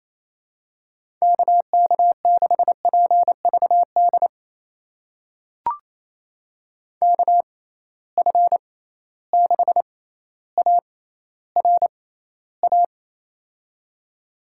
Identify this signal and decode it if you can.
Morse code, heard as a keyed tone, about 28 wpm, spaced as for 14 wpm, 700 Hz. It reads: KK6PVD KF6ARA
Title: USA Call Signs - 6 Characters - 6x Character Spacing - 28wpm